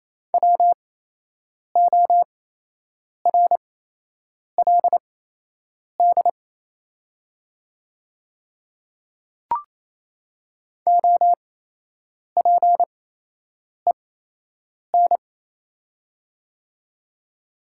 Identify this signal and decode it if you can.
Morse code, heard as a keyed tone, about 28 wpm, spaced as for 8 wpm, 700 Hz. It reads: WORLD OPEN
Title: 8x Character Spacing - Top 300 Words - 28wpm